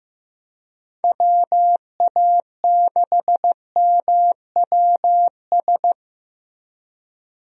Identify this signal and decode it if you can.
Morse code, heard as a keyed tone, about 15 wpm, 700 Hz. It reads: WA6MWS
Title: USA Call Signs - 6 Characters - Repeated 3x - 15wpm